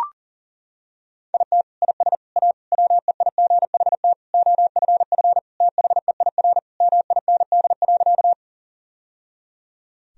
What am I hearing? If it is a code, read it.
Morse code, heard as a keyed tone, about 40 wpm, 700 Hz. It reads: IT IS A WEIGHT OFF THEIR MIND.
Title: Sentences from Top 1000 Words - Review Part 3 - 40wpm